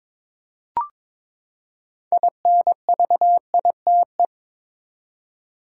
Morse code, heard as a keyed tone, about 22 wpm, 700 Hz. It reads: INVITE
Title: Six Letter Words - 22wpm